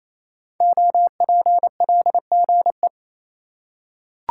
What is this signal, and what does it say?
Morse code, heard as a keyed tone, about 28 wpm, 690 Hz. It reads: OPLGE
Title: Code Groups - 5 Letters - 28wpm